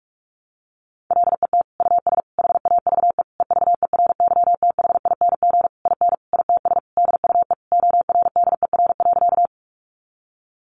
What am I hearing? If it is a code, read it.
Morse code, heard as a keyed tone, about 44 wpm, 700 Hz. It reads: LET US HAVE EVERYTHING IN ITS DUE ORDER.
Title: Sentences from Top 900 Words 45wpm